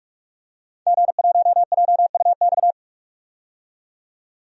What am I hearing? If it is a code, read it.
Morse code, heard as a keyed tone, about 46 wpm, 700 Hz. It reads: G1JUX
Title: DXCC-100 Call Signs - 5 Characters - 45wpm